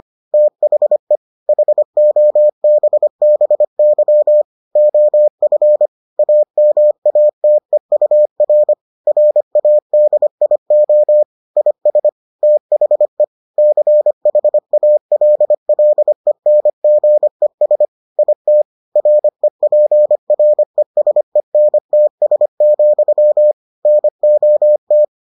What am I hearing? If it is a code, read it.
Morse code, heard as a keyed tone, about 25 wpm, 600 Hz. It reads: THE HOBBY OF AMATEUR RADIO IS THE CHALLENGES IT REPRESENTS, NOT